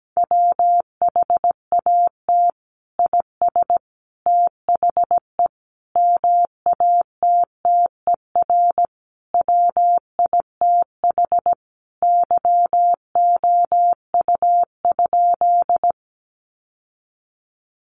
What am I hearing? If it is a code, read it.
Morse code, heard as a keyed tone, about 17 wpm, 700 Hz. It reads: WHAT IS THE MATTER WITH YOU?
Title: Sentences from Top 700 Words 17wpm